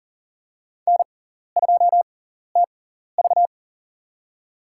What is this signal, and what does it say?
Morse code, heard as a keyed tone, about 40 wpm, spaced as for 14 wpm, 700 Hz. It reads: N2TV
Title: USA Call Signs - 6x Character Spacing - 40wpm